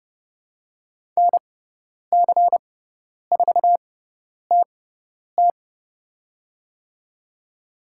Morse code, heard as a keyed tone, about 30 wpm, spaced as for 10 wpm, 700 Hz. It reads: NC4TT
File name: Lesson 16g - Letter c - Call Signs 30wpm